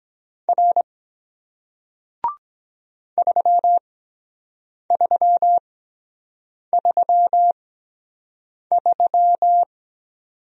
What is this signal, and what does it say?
Morse code, heard as a keyed tone, about 23 wpm, spaced as for 7 wpm, 700 Hz. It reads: R 3333